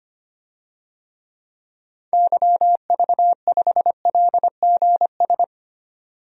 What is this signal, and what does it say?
Morse code, heard as a keyed tone, about 25 wpm, 700 Hz. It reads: YV5LGS
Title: DXCC-100 Call Signs - 6 Characters - 25wpm